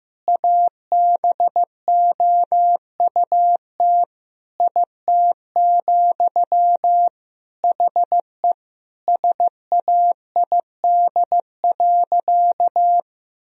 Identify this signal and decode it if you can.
Morse code, heard as a keyed tone, about 16 wpm, 700 Hz. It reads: ABOUT IT, HE SAID.
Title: Sentences from Top 1000 Words - Review Part 3 - 15wpm